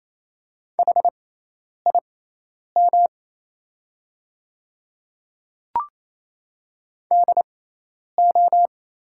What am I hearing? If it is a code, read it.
Morse code, heard as a keyed tone, about 28 wpm, spaced as for 10 wpm, 700 Hz. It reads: HIM DO
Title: 6x Character Spacing - Top 200 Words - 28wpm